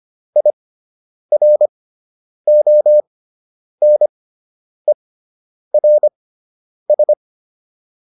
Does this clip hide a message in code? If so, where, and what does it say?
Yes: Morse code from 0.4 to 7.1 s, heard as a keyed tone, about 25 wpm, spaced as for 9 wpm, 600 Hz. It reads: IRONERS